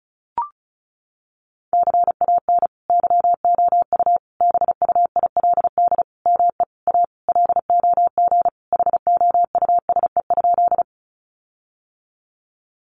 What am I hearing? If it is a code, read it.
Morse code, heard as a keyed tone, about 34 wpm, 700 Hz. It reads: CAN YOU BUILD ME A LOG HOUSE?
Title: Sentences from Top 1400 Words 35wpm